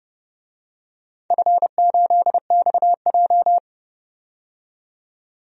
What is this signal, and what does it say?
Morse code, heard as a keyed tone, about 32 wpm, 700 Hz. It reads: F8XJ
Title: DXCC-100 Call Signs - 4 Characters - 30wpm